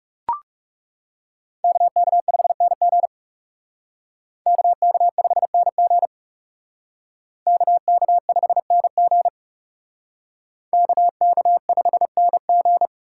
Morse code, heard as a keyed tone, about 36 wpm, 700 Hz. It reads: KK5NG KK5NG KK5NG KK5NG
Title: Speed Racing - USA 2x2 Call Signs 30wpm